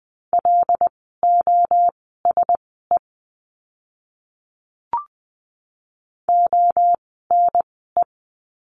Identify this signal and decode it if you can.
Morse code, heard as a keyed tone, about 20 wpm, spaced as for 15 wpm, 700 Hz. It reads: LOSE ONE
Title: 094 Lesson 16f - Letter c - Words 20wpm